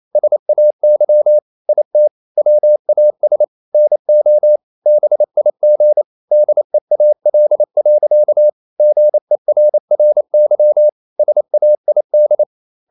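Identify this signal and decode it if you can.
Morse code, heard as a keyed tone, about 28 wpm, 600 Hz. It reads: SAY IT WAS NO BIG DEAL. GERRY SAID
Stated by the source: Morse code practice at 28 wpm, text of Ken Follett's novel Never in English, from CW med Gnister.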